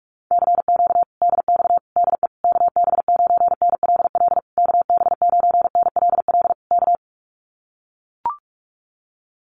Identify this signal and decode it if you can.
Morse code, heard as a keyed tone, about 44 wpm, 700 Hz. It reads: CQ DX DE KB9NLL KB9NLL K
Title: Common QSO Phrases Encore 45wpm